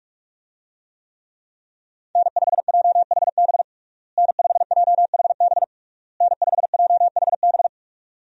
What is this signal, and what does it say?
Morse code, heard as a keyed tone, about 46 wpm, 700 Hz. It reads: N5JHB N5JHB N5JHB
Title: USA Call Signs - 5 Characters - Repeated 3x - 45wpm